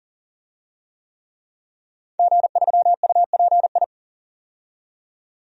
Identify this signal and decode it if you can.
Morse code, heard as a keyed tone, about 37 wpm, 700 Hz. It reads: G3UPI
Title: DXCC-100 Call Signs - 5 Characters - 40wpm